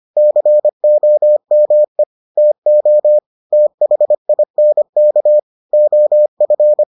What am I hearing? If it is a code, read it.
Morse code, heard as a keyed tone, about 25 wpm, 600 Hz. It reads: COME TO THINK OF